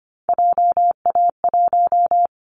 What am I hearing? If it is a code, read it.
Morse code, heard as a keyed tone, about 25 wpm, 700 Hz. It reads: JA1
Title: Callsign Prefixes for Top 50 DXCC - 25wpm